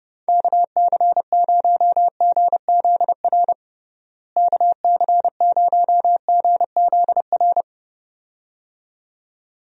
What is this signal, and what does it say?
Morse code, heard as a keyed tone, about 32 wpm, 700 Hz. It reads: KC0GZR KC0GZR
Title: USA Call Signs - 6 Characters - Repeated 2x - 30wpm